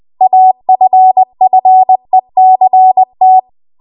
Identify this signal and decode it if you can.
Morse code, heard as a keyed tone, about 20 wpm, 750 Hz. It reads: AFFECT